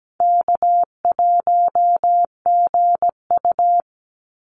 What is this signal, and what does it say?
Morse code, heard as a keyed tone, about 17 wpm, 700 Hz. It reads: K1GU
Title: Call Signs Repeated 2x - 17wpm